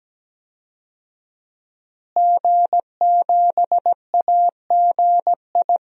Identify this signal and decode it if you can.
Morse code, heard as a keyed tone, about 17 wpm, 700 Hz. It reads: G7AGI
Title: DXCC-100 Call Signs - 5 Characters - Repeated 3x - 17wpm